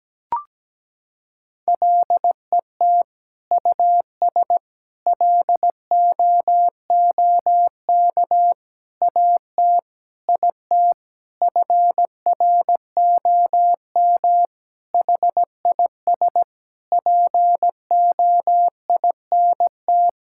Morse code, heard as a keyed tone, about 17 wpm, 700 Hz. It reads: LET US LOOK AT IT FROM HIS POINT
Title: Sentences from Top 1000 Words - Review Part 6 - 17wpm